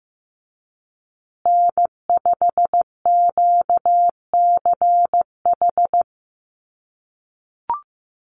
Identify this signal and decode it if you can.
Morse code, heard as a keyed tone, about 15 wpm, 700 Hz. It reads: N5QCH
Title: USA Call Signs Repeated 3x 15wpm